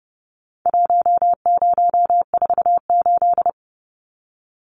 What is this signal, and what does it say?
Morse code, heard as a keyed tone, about 30 wpm, 700 Hz. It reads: 1048
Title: Numbers - Contest Style - 4-Digit 30wpm